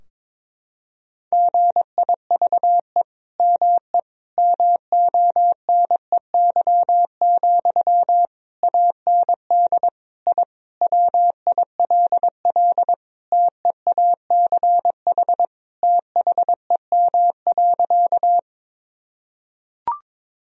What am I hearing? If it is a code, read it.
Morse code, heard as a keyed tone, about 22 wpm, 700 Hz. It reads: GIVE ME MONEY, AND I WILL TEACH THEM.
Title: Sentences from Top 1000 Words - Review Part 2 - 22wpm